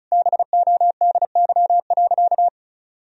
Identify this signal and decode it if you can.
Morse code, heard as a keyed tone, about 36 wpm, 700 Hz. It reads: BODY.
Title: Sentences from Top 1000 Words - Review Part 2 - 35wpm